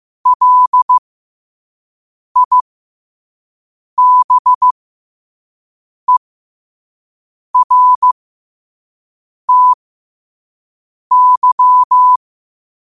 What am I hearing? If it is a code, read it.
Morse code, heard as a keyed tone, about 15 wpm, spaced as for 5 wpm, 1000 Hz. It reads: LIBERTY
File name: qotdfiles / 5wpm